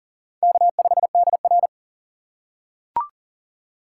Morse code, heard as a keyed tone, about 40 wpm, 700 Hz. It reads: K5DR
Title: USA Call Signs - 4 Characters - 6x Character Spacing - 40wpm